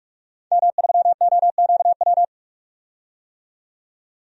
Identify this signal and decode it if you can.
Morse code, heard as a keyed tone, about 44 wpm, 700 Hz. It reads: M3OQW
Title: DXCC-100 Call Signs - 5 Characters - 45wpm